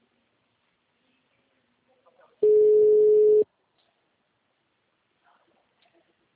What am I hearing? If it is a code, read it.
Ringback tone (Europe, 425 Hz).